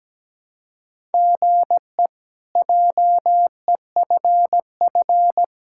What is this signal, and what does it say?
Morse code, heard as a keyed tone, about 17 wpm, 700 Hz. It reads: GE JEFF
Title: K1USN SST QSO Practice - 17wpm